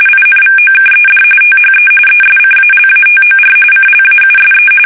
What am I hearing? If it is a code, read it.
RTTY, about 45.45 baud. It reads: INVOKED THE ESCALATOR CLAUSE
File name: fsk850 / battleshipnj